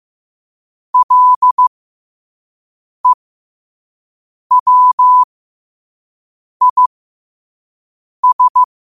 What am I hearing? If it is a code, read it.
Morse code, heard as a keyed tone, about 15 wpm, spaced as for 5 wpm, 1000 Hz. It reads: LEWIS
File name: qotdfiles / 5wpm